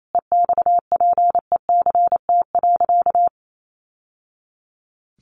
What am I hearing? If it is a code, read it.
Morse code, heard as a keyed tone, about 28 wpm, 700 Hz. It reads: EXPECT.
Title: Sentences from Top 1100 Words 28wpm